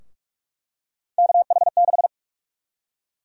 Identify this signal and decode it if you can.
Morse code, heard as a keyed tone, about 42 wpm, 700 Hz. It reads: KH6
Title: USA Call Sign Prefixes - 45wpm